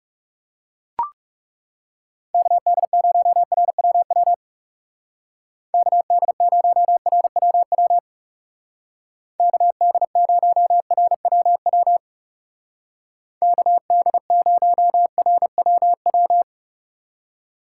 Morse code, heard as a keyed tone, about 36 wpm, 700 Hz. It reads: KD0RWW KD0RWW KD0RWW KD0RWW
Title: Speed Racing - USA 2x3 Call Signs 30wpm